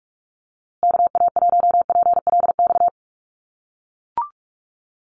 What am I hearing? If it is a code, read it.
Morse code, heard as a keyed tone, about 44 wpm, 700 Hz. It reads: KA1PLX
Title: USA Call Signs Repeated 3x 45wpm